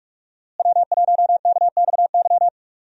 Morse code, heard as a keyed tone, about 46 wpm, 700 Hz. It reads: W1KXY
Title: USA Call Signs - 5 Characters - Repeated 3x - 45wpm